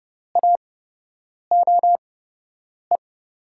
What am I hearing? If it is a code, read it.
Morse code, heard as a keyed tone, about 30 wpm, spaced as for 8 wpm, 700 Hz. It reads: AOE
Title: USA Call Signs - 8x Character Spacing - 30wpm